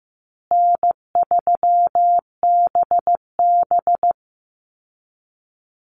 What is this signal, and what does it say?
Morse code, heard as a keyed tone, about 15 wpm, 700 Hz. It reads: N3BB
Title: Easy Call Signs Repeated 3x - 15wpm